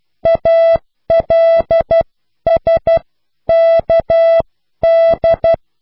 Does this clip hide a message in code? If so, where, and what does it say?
Yes: Morse code from 0.3 to 5.5 s, heard as a keyed tone, about 12 wpm, spaced as for 10 wpm, 660 Hz. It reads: ALSKD